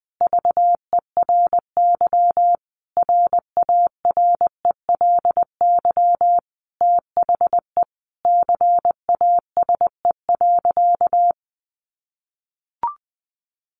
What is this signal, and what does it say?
Morse code, heard as a keyed tone, about 20 wpm, 700 Hz. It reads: VERY RARELY THE CASE.
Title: Sentences from Top 1400 Words 20wpm